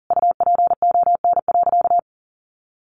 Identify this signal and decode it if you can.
Morse code, heard as a keyed tone, about 40 wpm, 700 Hz. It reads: UPON.
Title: Sentences from Top 1100 Words 40wpm